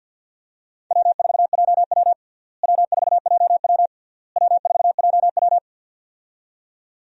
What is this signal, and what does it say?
Morse code, heard as a keyed tone, about 51 wpm, 700 Hz. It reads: W4JW W4JW W4JW
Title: USA Call Signs - 4 Characters - Repeated 3x - 50wpm